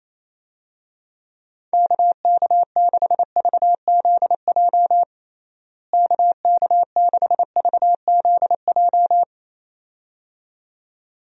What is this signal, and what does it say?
Morse code, heard as a keyed tone, about 28 wpm, 700 Hz. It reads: KK6VZJ KK6VZJ
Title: USA Call Signs - 6 Characters - Repeated 2x - 28wpm